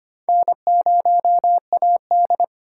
Morse code, heard as a keyed tone, about 30 wpm, 700 Hz. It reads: N0AD
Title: Speed Racing - USA 1x2 Call Signs 25wpm